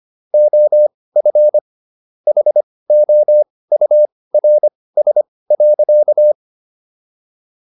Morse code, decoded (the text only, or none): OF HOURS.